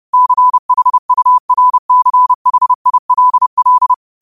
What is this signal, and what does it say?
Morse code, heard as a keyed tone, about 30 wpm, 1000 Hz. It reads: CHURCHILL